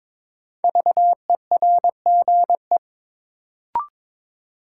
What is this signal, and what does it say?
Morse code, heard as a keyed tone, about 22 wpm, 700 Hz. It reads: VERGE
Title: Sets of 4 Words - New Word Drill - 22wpm